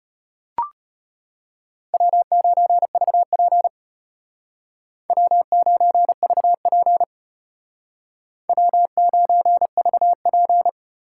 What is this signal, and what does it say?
Morse code, heard as a keyed tone, about 32 wpm, 700 Hz. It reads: W9VP W9VP W9VP